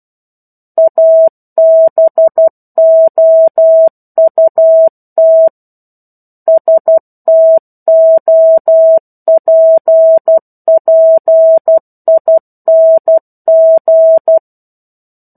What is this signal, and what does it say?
Morse code, heard as a keyed tone, about 12 wpm, 650 Hz. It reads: ABOUT STOPPING
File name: 01-Chapter-01 12 WPM